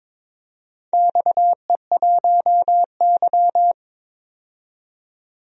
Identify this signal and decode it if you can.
Morse code, heard as a keyed tone, about 22 wpm, 700 Hz. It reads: XE1Y
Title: DXCC-100 Call Signs - 4 Characters - 22wpm